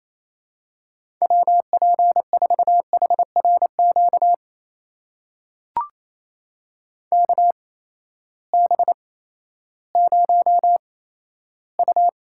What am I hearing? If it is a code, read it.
Morse code, heard as a keyed tone, about 27 wpm, spaced as for 14 wpm, 700 Hz. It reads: WP4HRQ KB0U
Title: USA Call Signs - 6 Characters - 8x Character Spacing - 28wpm